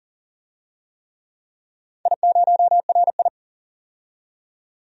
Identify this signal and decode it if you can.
Morse code, heard as a keyed tone, about 40 wpm, 700 Hz. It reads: I0RI
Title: DXCC-100 Call Signs - 4 Characters - 40wpm